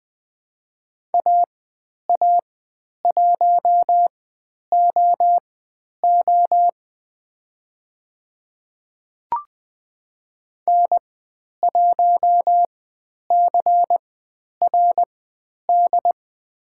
Morse code, heard as a keyed tone, about 20 wpm, spaced as for 10 wpm, 700 Hz. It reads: AA1OO N1CRD
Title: 095 Lesson 16g - Letter c - Call Signs 20wpm